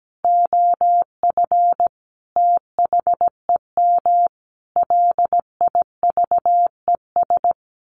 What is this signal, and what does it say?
Morse code, heard as a keyed tone, about 17 wpm, 700 Hz. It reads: OF THEM LIVES
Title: A Sound Of Thunder 17wpm